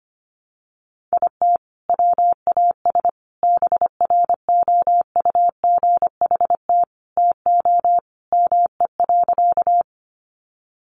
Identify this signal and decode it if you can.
Morse code, heard as a keyed tone, about 25 wpm, 700 Hz. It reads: IT WAS BROUGHT TO ME.
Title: Sentences from Top 500 Words 25wpm